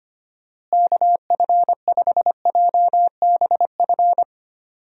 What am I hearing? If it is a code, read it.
Morse code, heard as a keyed tone, about 25 wpm, 700 Hz. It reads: KF5JBF
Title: USA Call Signs - 6 Characters - Repeated 2x - 25wpm